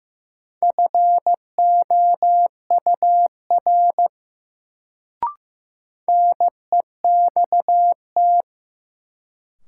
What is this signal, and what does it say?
Morse code, heard as a keyed tone, about 15 wpm, 700 Hz. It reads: FOUR NEXT
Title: Sentences from Top 300 Words - New Word Drill - 15wpm